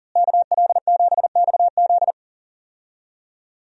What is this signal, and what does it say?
Morse code, heard as a keyed tone, about 40 wpm, 700 Hz. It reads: KL7XZ
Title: DXCC-100 Call Signs - 5 Characters - Repeated 3x - 40wpm